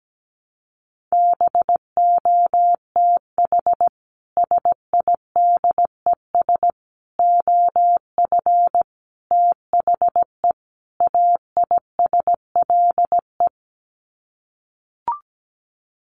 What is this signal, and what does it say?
Morse code, heard as a keyed tone, about 17 wpm, 700 Hz. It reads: BOTH SIDES OF THE AISLE
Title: Sets of 5 Words Encore2 17wpm